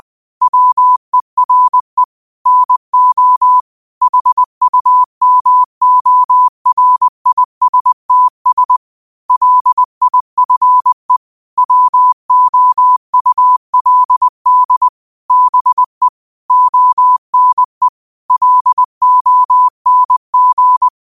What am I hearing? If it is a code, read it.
Morse code, heard as a keyed tone, about 20 wpm, 1000 Hz. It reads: WERE NO HUMORISTS LIFE WOULD BE ONE LONG
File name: qotdfiles / 20wpm